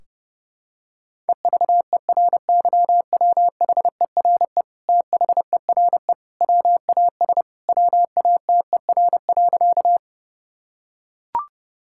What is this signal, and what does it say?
Morse code, heard as a keyed tone, about 32 wpm, 700 Hz. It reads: EVERYWHERE THERE WAS WATER.
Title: Sentences from Top 1600 Words 30wpm